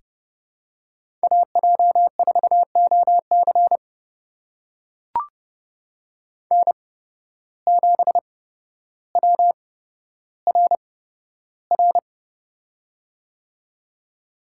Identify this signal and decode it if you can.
Morse code, heard as a keyed tone, about 30 wpm, spaced as for 13 wpm, 700 Hz. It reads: AJ4OC N7WRR
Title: USA Call Signs - 8x Character Spacing - 30wpm